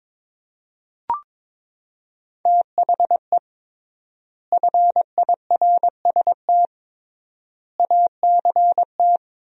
Morse code, heard as a keyed tone, about 22 wpm, 700 Hz. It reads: THE FIRST ACT
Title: 3x Word-Spacing - Sentences from Top 200 Words - 22wpm